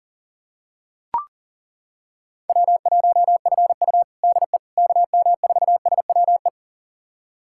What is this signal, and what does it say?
Morse code, heard as a keyed tone, about 40 wpm, 700 Hz. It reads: W1FU DE KM4SWE
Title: USA Callsign DE Callsign 40wpm